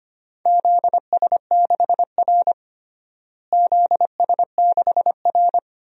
Morse code, heard as a keyed tone, about 25 wpm, 700 Hz. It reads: ZS6R ZS6R
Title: DXCC-100 Call Signs - 4 Characters - Repeated 2x - 25wpm